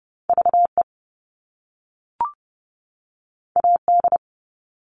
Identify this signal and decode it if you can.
Morse code, heard as a keyed tone, about 30 wpm, 700 Hz. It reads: V E A D
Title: All Cut Numbers - 2-Digit 30wpm